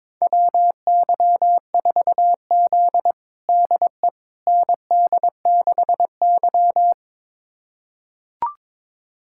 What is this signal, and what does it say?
Morse code, heard as a keyed tone, about 21 wpm, 700 Hz. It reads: WY4Z DE ND6Y
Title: USA Callsign DE Callsign - 4 Characters - 22wpm